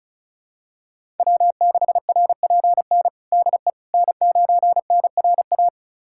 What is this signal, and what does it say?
Morse code, heard as a keyed tone, about 36 wpm, 700 Hz. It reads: W6RPN DE N9NRA
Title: USA Callsign DE Callsign - 5 Characters - 35wpm